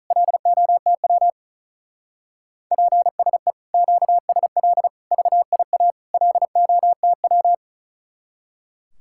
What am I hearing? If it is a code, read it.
Morse code, heard as a keyed tone, about 45 wpm, 700 Hz. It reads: LOTW PSE QSL VIA LOTW